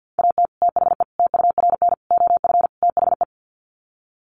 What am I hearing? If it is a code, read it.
Morse code, heard as a keyed tone, about 50 wpm, 700 Hz. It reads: AT THE TURN OF THE
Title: Sets of 5 Words 50wpm